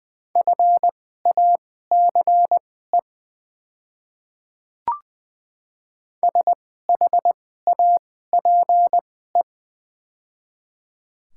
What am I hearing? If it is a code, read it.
Morse code, heard as a keyed tone, about 21 wpm, spaced as for 15 wpm, 700 Hz. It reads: FACE SHAPE
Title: 2x Character Spacing - Top 500 Words - 20wpm